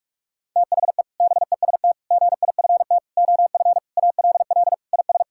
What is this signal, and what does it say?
Morse code, heard as a keyed tone, about 46 wpm, 700 Hz. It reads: THE BEST GIFT OF ALL IS
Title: Sentences from Top 1000 Words - Review Part 6 - 45wpm